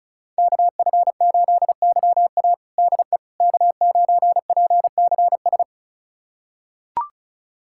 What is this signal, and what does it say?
Morse code, heard as a keyed tone, about 36 wpm, 700 Hz. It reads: KF8YA DE K9PCS
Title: USA Callsign DE Callsign - 5 Characters - 35wpm